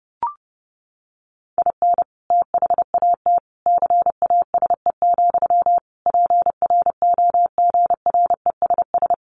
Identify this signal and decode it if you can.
Morse code, heard as a keyed tone, about 30 wpm, 700 Hz. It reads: IN THAT CASE, PROGRESS
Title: Sentences from Top 1400 Words 30wpm